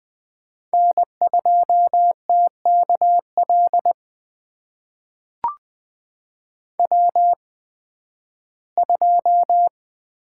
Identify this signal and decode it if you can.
Morse code, heard as a keyed tone, about 20 wpm, spaced as for 13 wpm, 700 Hz. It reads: N2TKL W2
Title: USA Call Signs - 5 Characters - 8x Character Spacing - 20wpm